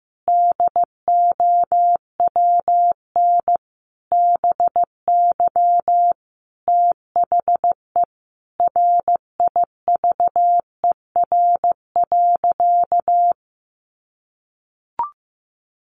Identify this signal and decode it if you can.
Morse code, heard as a keyed tone, about 15 wpm, 700 Hz. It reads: DOWN BY THE RIVER.
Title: Sentences from Top 1200 Words 15wpm